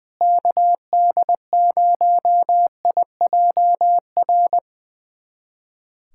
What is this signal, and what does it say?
Morse code, heard as a keyed tone, about 20 wpm, 700 Hz. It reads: KD0IJR
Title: USA Callsign DE Callsign - 6 Characters - 20wpm